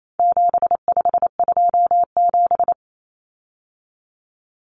Morse code, heard as a keyed tone, about 28 wpm, 700 Hz. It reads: 7527
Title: Numbers - 4-Digit 28wpm